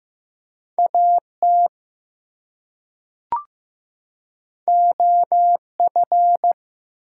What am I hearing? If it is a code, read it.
Morse code, heard as a keyed tone, about 15 wpm, 700 Hz. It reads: AT OF
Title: Two Letter Words - 15wpm